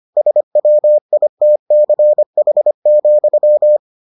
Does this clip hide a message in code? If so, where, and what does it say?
Yes: Morse code from 0.2 to 3.8 s, heard as a keyed tone, about 25 wpm, 600 Hz. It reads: SWITCH,